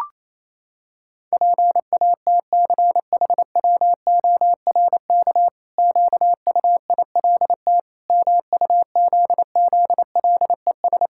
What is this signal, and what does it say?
Morse code, heard as a keyed tone, about 28 wpm, 700 Hz. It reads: PATCHWORK QUILT MUZZLES
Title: Pangram Challenge v2 - 28wpm